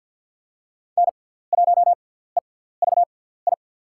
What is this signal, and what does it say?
Morse code, heard as a keyed tone, about 48 wpm, spaced as for 17 wpm, 700 Hz. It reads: N1EVI
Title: USA Call Signs - 5 Characters - 6x Character Spacing - 50wpm